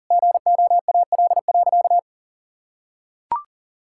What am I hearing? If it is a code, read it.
Morse code, heard as a keyed tone, about 40 wpm, 700 Hz. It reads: GOAL.
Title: Sentences from Top 1000 Words - Review Part 5 - 40wpm